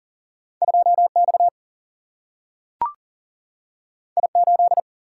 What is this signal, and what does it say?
Morse code, heard as a keyed tone, about 40 wpm, 700 Hz. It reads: 2 X I 8
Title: Single-Letter Single-Number 40wpm